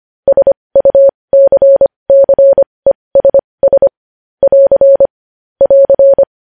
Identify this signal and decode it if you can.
Morse code, heard as a keyed tone, about 25 wpm, spaced as for 20 wpm, 550 Hz. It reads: SUCCESS <AR> <AR>
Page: News Headlines in Morse Code at 20 WPM